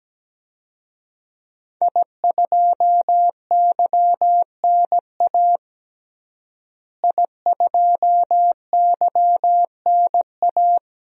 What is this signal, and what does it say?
Morse code, heard as a keyed tone, about 17 wpm, 700 Hz. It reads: I2YNA I2YNA
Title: DXCC-100 Call Signs - 5 Characters - Repeated 2x - 17wpm